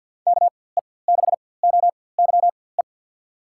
Morse code, heard as a keyed tone, about 45 wpm, spaced as for 23 wpm, 700 Hz. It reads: KE6OYE
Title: USA Call Signs - 4x Character Spacing - 50wpm